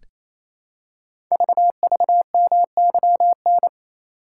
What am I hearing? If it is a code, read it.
Morse code, heard as a keyed tone, about 28 wpm, 690 Hz. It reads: VVMYN
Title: Code Groups - 5 Letters - 28wpm